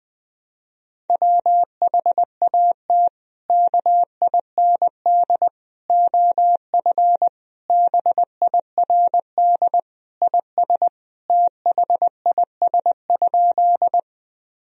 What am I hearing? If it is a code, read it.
Morse code, heard as a keyed tone, about 21 wpm, 700 Hz. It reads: WHAT KIND OF BIRD IS THIS?
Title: Sentences from Top 1000 Words - Review Part 2 - 20wpm